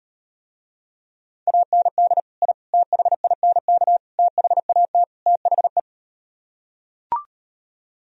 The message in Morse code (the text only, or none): AND I THINK THAT THE